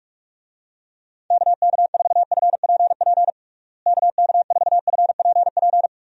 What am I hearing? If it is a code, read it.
Morse code, heard as a keyed tone, about 46 wpm, 700 Hz. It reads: KK4FPP KK4FPP
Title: USA Call Signs - 6 Characters - Repeated 3x - 45wpm